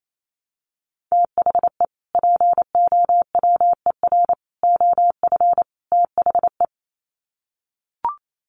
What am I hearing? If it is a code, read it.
Morse code, heard as a keyed tone, about 28 wpm, 700 Hz. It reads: THE POWER OF THE
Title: Sets of 4 Words 28wpm